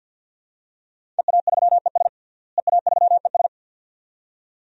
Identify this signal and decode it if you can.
Morse code, heard as a keyed tone, about 51 wpm, 700 Hz. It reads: EA3ES EA3ES
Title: DXCC-100 Call Signs - 5 Characters - Repeated 2x - 50wpm